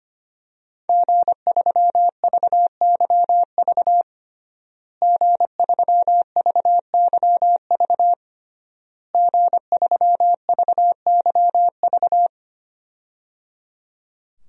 Morse code, heard as a keyed tone, about 25 wpm, 700 Hz. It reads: G3VYV G3VYV G3VYV
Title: DXCC-100 Call Signs - 5 Characters - Repeated 3x - 25wpm